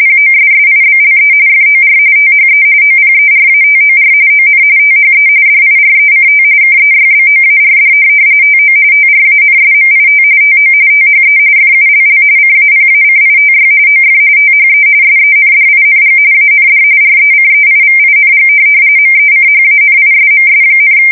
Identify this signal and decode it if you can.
RTTY, about 45.45 baud. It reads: POSSIBLE FOR THE OPERATOR ON THE DIRECT KEYBOARD TO TRANSMIT TO THE FULL CAPACITY OF THE SENDING AND RECEIVING MECHANISMS,
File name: fsk170 / printing-telegraph